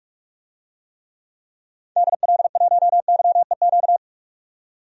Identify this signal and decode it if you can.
Morse code, heard as a keyed tone, about 46 wpm, 700 Hz. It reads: DL1YEQ
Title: DXCC-100 Call Signs - 6 Characters - 45wpm